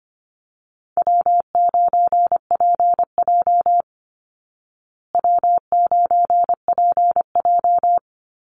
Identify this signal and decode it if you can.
Morse code, heard as a keyed tone, about 25 wpm, 700 Hz. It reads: W9PJ W9PJ
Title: Easy Call Signs Repeated 2x - 25wpm